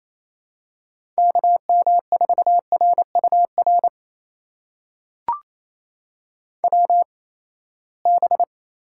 Morse code, heard as a keyed tone, about 28 wpm, spaced as for 19 wpm, 700 Hz. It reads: KM4RUR WB
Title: USA Call Signs - 6 Characters - 8x Character Spacing - 28wpm